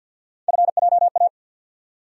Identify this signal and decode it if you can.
Morse code, heard as a keyed tone, about 50 wpm, 700 Hz. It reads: FJA
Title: USA Call Sign Postfixes - 3 Letters 50wpm